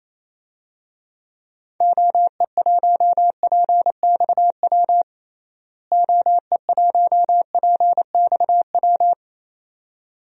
Morse code, heard as a keyed tone, about 27 wpm, 700 Hz. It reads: OE1PXW OE1PXW
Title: DXCC-100 Call Signs - 6 Characters - Repeated 2x - 28wpm